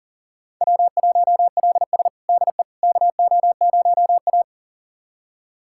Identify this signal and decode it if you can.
Morse code, heard as a keyed tone, about 40 wpm, 700 Hz. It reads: W1LS DE KO0A
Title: USA Callsign DE Callsign - 4 Characters - 40wpm